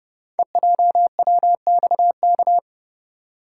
Morse code, heard as a keyed tone, about 28 wpm, 700 Hz. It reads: EJWXK
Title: Code Groups - 5 Letters - 30wpm